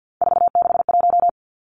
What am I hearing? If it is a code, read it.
Morse code, heard as a keyed tone, about 50 wpm, 700 Hz. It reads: 461
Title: Three Challenging Letter-Number Combinations 50wpm